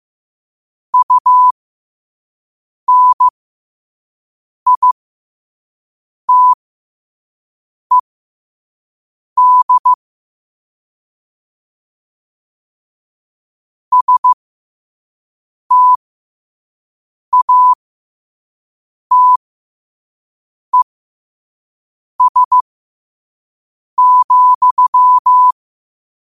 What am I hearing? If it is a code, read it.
Morse code, heard as a keyed tone, about 15 wpm, spaced as for 5 wpm, 1000 Hz. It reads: UNITED STATES,
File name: qotdfiles / 5wpm